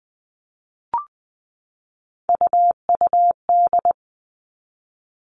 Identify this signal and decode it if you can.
Morse code, heard as a keyed tone, about 20 wpm, 700 Hz. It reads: UUD
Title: All Cut Numbers - 3-Digit 20wpm